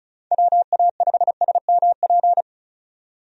Morse code, heard as a keyed tone, about 36 wpm, 700 Hz. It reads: WA5SMP
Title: USA Call Signs - 6 Characters - 35wpm